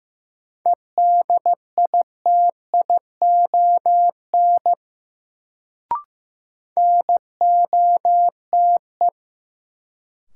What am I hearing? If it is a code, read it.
Morse code, heard as a keyed tone, about 15 wpm, 700 Hz. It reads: EDITION NOTE
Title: Sets of 5 Words - New Word Drill - 15wpm